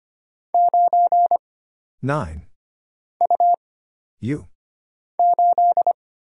Morse code, heard as a keyed tone, about 25 wpm, 700 Hz. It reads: 9 U 8
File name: Single Character (Letter:Number:Punctuation:Prosign) - Mind-Melt - 25wpm